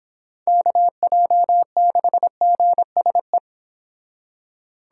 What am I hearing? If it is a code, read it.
Morse code, heard as a keyed tone, about 29 wpm, 700 Hz. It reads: KJ6GSE